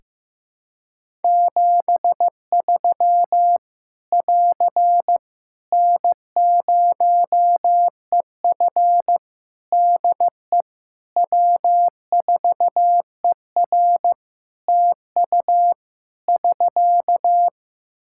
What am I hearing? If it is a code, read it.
Morse code, heard as a keyed tone, about 15 wpm, 700 Hz. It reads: 73 <AR> N0EF DE W4ER TU <SK>
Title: Practice QSO - Exchange 4 - Ending - 15wpm